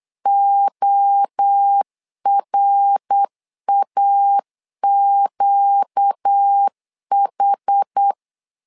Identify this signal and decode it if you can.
Morse code, heard as a keyed tone, about 8 wpm, 800 Hz. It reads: ORAQH
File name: LEZIONI / 4 Q 8wpm 2732